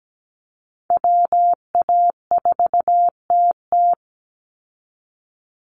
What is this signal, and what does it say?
Morse code, heard as a keyed tone, about 17 wpm, 700 Hz. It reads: WA4TT
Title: Call Signs Repeated 3x - 17wpm